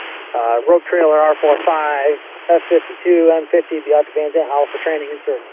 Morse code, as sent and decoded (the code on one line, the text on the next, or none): none